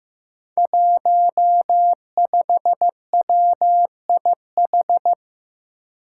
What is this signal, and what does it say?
Morse code, heard as a keyed tone, about 15 wpm, 700 Hz. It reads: 15WIH
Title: Five Challenging Letter-Number Combinations 15wpm